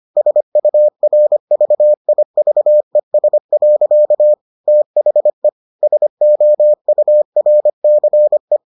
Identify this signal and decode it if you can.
Morse code, heard as a keyed tone, about 25 wpm, 600 Hz. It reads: SURVIVES. THE SOURCE